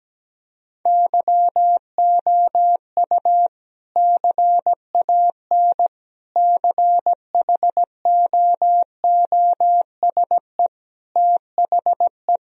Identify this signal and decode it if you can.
Morse code, heard as a keyed tone, about 17 wpm, 700 Hz. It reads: YOU CAN CHOOSE THE
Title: Sentences from Top 1000 Words - Review Part 4 - 17wpm